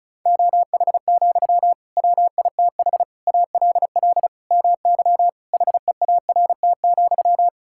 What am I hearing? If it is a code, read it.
Morse code, heard as a keyed tone, about 34 wpm, 700 Hz. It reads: OH, WITH ALL MY HEART,
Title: Sentences from Top 1600 Words 35wpm